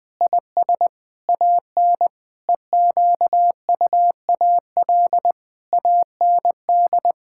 Morse code, heard as a keyed tone, about 20 wpm, 700 Hz. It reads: IS AN EQUAL AND
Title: Sentences with 8-Word Phrases - 20wpm